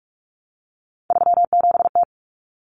Morse code, heard as a keyed tone, about 44 wpm, 700 Hz. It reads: 37T
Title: Common Cut Numbers - 3-Digit 45wpm